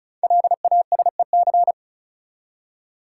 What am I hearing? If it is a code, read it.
Morse code, heard as a keyed tone, about 36 wpm, 700 Hz. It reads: LASEC
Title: Code Groups - 5 Letters - 35wpm